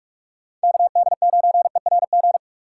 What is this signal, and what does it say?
Morse code, heard as a keyed tone, about 46 wpm, 700 Hz. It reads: KD9ERG
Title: USA Call Signs - 6 Characters - 4x Character Spacing - 45wpm